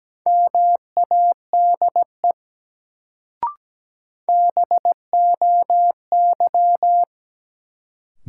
Morse code, heard as a keyed tone, about 17 wpm, 700 Hz. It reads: MADE BOY
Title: Sentences from Top 200 Words - New Word Drill - 17wpm